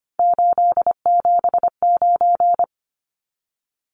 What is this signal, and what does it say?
Morse code, heard as a keyed tone, about 25 wpm, 700 Hz. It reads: 879
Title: Numbers - 3-Digit 25wpm